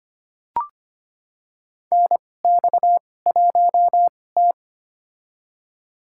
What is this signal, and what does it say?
Morse code, heard as a keyed tone, about 25 wpm, spaced as for 18 wpm, 700 Hz. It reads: NX1T
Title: USA Call Signs - 2x Character Spacing - 25wpm